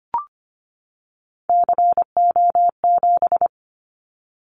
Morse code, heard as a keyed tone, about 25 wpm, 700 Hz. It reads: CO7
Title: Callsign Prefixes for Top 100 DXCC - 25wpm